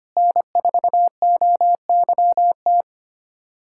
Morse code, heard as a keyed tone, about 25 wpm, 700 Hz. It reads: N4OYT
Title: USA Callsign DE Callsign - 5 Characters - 25wpm